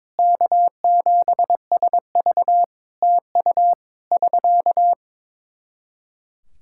Morse code, heard as a keyed tone, about 22 wpm, 700 Hz. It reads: K7SV TU <SK>
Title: K1USN SST QSO Practice 22wpm